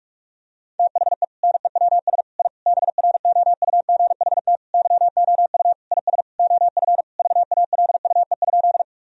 Morse code, heard as a keyed tone, about 46 wpm, 700 Hz. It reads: THE NEWS I BROUGHT YOU IS OF VALUE?
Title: Sentences from Top 1000 Words - Review Part 3 - 45wpm